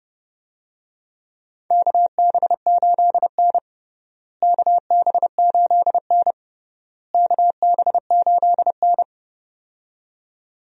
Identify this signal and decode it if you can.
Morse code, heard as a keyed tone, about 30 wpm, 700 Hz. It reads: KB8N KB8N KB8N
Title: USA Call Signs - 4 Characters - Repeated 3x - 30wpm